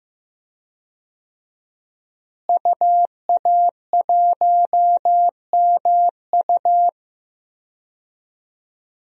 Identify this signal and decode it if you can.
Morse code, heard as a keyed tone, about 15 wpm, 700 Hz. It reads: UA1MU
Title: DXCC-100 Call Signs - 5 Characters - Repeated 3x - 15wpm